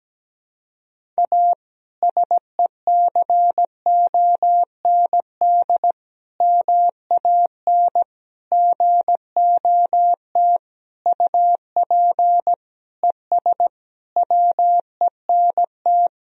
Morse code, heard as a keyed tone, about 17 wpm, 700 Hz. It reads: A SECOND MAN GOT UP ES WENT
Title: Sentences from Top 300 Words and Abbreviations 17wpm